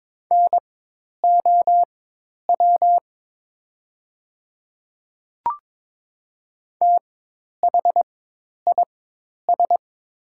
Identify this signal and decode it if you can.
Morse code, heard as a keyed tone, about 22 wpm, spaced as for 10 wpm, 700 Hz. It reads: NOW THIS
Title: 4x Character Spacing - Top 200 Words - 22wpm